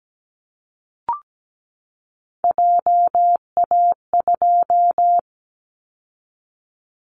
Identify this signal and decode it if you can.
Morse code, heard as a keyed tone, about 17 wpm, 700 Hz. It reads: JA2
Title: Callsign Prefixes for Top 50 DXCC - 17wpm